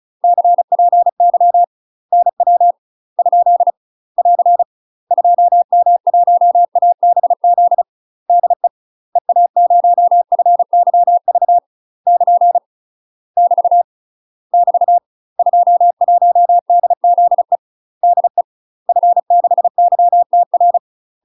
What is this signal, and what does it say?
Morse code, heard as a keyed tone, about 36 wpm, 700 Hz. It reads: CPY NW ? <AR> 2M1ABZ DE EA0FYV <KN> <BT> <BT> 21DZE DE F6YTR
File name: UK MASTER QSOs TRAINING / 35WPM / UK MASTER QSOs 0002